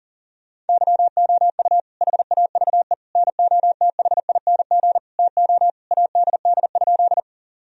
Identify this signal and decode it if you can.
Morse code, heard as a keyed tone, about 40 wpm, 700 Hz. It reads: YOU HAVE NOTHING TO ADD?
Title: Sentences from Top 1000 Words - Review Part 1 - 40wpm